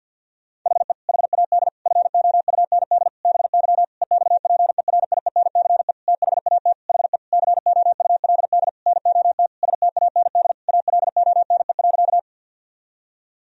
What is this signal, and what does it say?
Morse code, heard as a keyed tone, about 49 wpm, 700 Hz. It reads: HE HAD FOUND BY EXPERIENCE THAT HE COULD NOT STAND ALONE.
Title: Sentences from Top 1000 Words - Review Part 3 - 50wpm